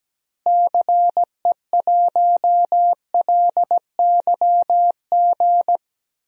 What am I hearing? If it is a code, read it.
Morse code, heard as a keyed tone, about 17 wpm, 700 Hz. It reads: CE1LYG
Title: DXCC-100 Call Signs - 6 Characters - Repeated 3x - 17wpm